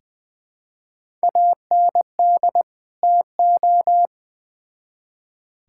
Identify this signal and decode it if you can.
Morse code, heard as a keyed tone, about 20 wpm, 700 Hz. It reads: AND TO
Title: Build A Phrase - 2-Word Phrases - 20wpm